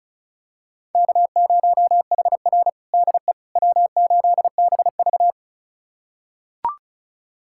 Morse code, heard as a keyed tone, about 34 wpm, 700 Hz. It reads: K0HR DE W8BV
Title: USA Callsign DE Callsign - 4 Characters - 35wpm